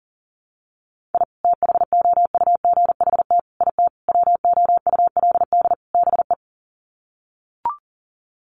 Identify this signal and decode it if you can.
Morse code, heard as a keyed tone, about 40 wpm, 700 Hz. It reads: I THOUGHT IT WOULD BE
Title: Sets of 5 Words Encore 40wpm